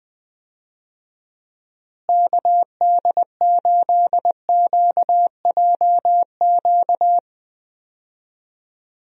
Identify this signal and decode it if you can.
Morse code, heard as a keyed tone, about 20 wpm, 700 Hz. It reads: KD8QJQ
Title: USA Call Signs - 6 Characters - Repeated 2x - 20wpm